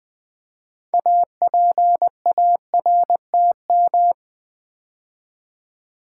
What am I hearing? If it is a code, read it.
Morse code, heard as a keyed tone, about 20 wpm, 700 Hz. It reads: APARTM
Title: Build A Word - 3 Letters At A Time - 9-Letter Words - 20wpm